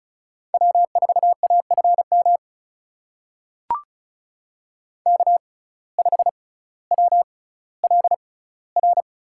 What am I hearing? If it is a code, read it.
Morse code, heard as a keyed tone, about 36 wpm, spaced as for 18 wpm, 700 Hz. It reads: W4AFM K5WLR
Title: USA Call Signs - 5 Characters - 6x Character Spacing - 35wpm